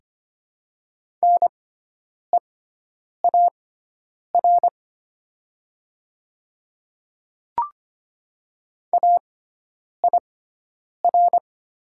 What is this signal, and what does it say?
Morse code, heard as a keyed tone, about 25 wpm, spaced as for 9 wpm, 700 Hz. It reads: NEAR AIR
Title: 6x Character Spacing - Top 300 Words - 25wpm